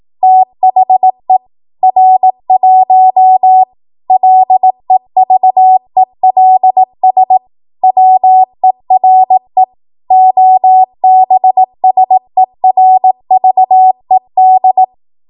Morse code, heard as a keyed tone, about 18 wpm, 750 Hz. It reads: THE R1 LEVELS WERE OBSERVED
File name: Morse / 18WPM